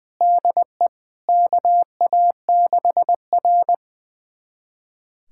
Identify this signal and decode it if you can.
Morse code, heard as a keyed tone, about 20 wpm, 700 Hz. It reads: DE KA6R
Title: USA Callsign DE Callsign - 4 Characters - 20wpm